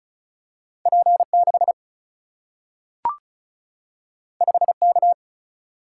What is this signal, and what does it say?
Morse code, heard as a keyed tone, about 36 wpm, 700 Hz. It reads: P6 5K
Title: Single-Letter Single-Number 35wpm